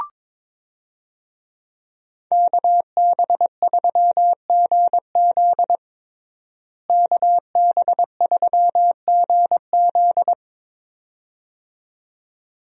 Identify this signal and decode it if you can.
Morse code, heard as a keyed tone, about 22 wpm, 700 Hz. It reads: KB3GZ KB3GZ
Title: USA Call Signs - 5 Characters - Repeated 2x - 22wpm